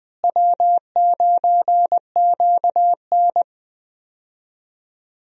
Morse code, heard as a keyed tone, about 21 wpm, 700 Hz. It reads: W9QN
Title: USA Call Signs - 4 Characters - Repeated 2x - 20wpm